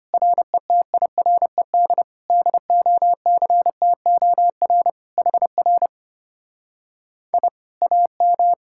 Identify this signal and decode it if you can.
Morse code, heard as a keyed tone, about 33 wpm, 700 Hz. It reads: RETIRED DOCTOR HR I AM